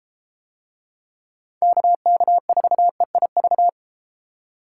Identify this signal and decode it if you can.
Morse code, heard as a keyed tone, about 32 wpm, 700 Hz. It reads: KK4EIV